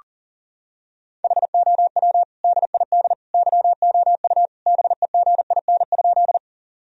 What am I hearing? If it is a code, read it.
Morse code, heard as a keyed tone, about 40 wpm, 700 Hz. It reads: HOW DID YOU BEGIN?
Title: Sentences from Top 1000 Words - Review Part 1 - 40wpm